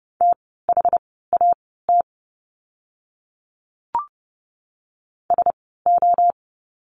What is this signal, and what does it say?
Morse code, heard as a keyed tone, about 30 wpm, spaced as for 17 wpm, 700 Hz. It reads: THAT SO
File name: course / 30:10wpm / 230 Lesson 37f - Letter x - Words 30wpm